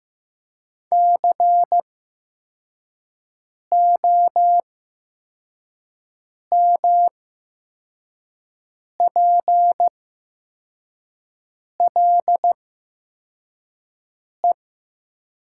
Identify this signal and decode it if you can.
Morse code, heard as a keyed tone, about 15 wpm, spaced as for 4 wpm, 700 Hz. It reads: COMPLE
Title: 8x Character Spacing - Top 500 Words - 15wpm